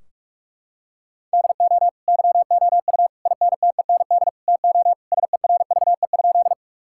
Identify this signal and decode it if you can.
Morse code, heard as a keyed tone, about 46 wpm, 700 Hz. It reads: DO YOU INTEND TO SERVE?
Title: Sentences from Top 1600 Words 45wpm